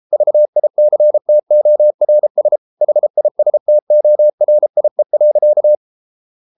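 Morse code, heard as a keyed tone, about 33 wpm, 600 Hz. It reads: VICTORS HISTORIE.
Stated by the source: Morse code at 33 wpm, QRQ.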